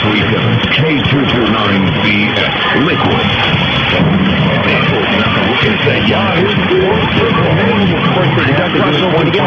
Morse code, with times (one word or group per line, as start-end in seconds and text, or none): none